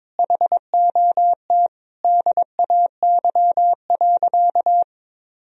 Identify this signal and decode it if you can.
Morse code, heard as a keyed tone, about 22 wpm, 700 Hz. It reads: HOT DAY.
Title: Sentences from Top 1000 Words - Review Part 3 - 22wpm